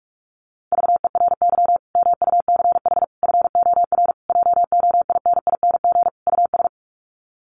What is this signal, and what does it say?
Morse code, heard as a keyed tone, about 44 wpm, 700 Hz. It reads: VERY MUCH FOR JOINING US
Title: Sets of 5 Words 45wpm